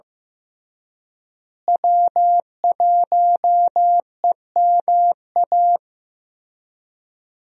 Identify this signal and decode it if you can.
Morse code, heard as a keyed tone, about 15 wpm, 700 Hz. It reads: W1EMA
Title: USA Call Signs - 5 Characters - Repeated 3x - 15wpm